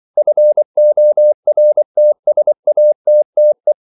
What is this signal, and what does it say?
Morse code, heard as a keyed tone, about 24 wpm, 600 Hz. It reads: FORTSATTE
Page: Ferie 24 wpm